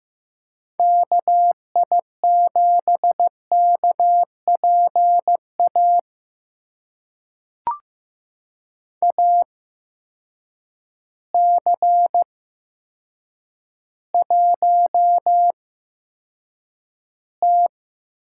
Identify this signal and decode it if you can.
Morse code, heard as a keyed tone, about 15 wpm, spaced as for 8 wpm, 700 Hz. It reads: KI7KPA AC1T
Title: USA Call Signs - 8x Character Spacing - 15wpm